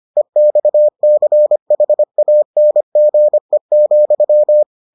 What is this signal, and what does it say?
Morse code, heard as a keyed tone, about 25 wpm, 600 Hz. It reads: EXCHANGE,